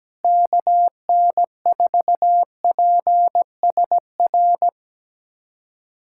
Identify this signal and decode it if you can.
Morse code, heard as a keyed tone, about 17 wpm, 700 Hz. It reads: KN4PSR
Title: USA Callsign DE Callsign - 6 Characters - 17wpm